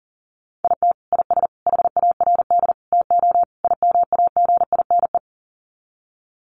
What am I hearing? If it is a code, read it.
Morse code, heard as a keyed tone, about 40 wpm, 700 Hz. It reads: IT IS HARD TO IMAGINE
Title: Sets of 5 Words Encore2 40wpm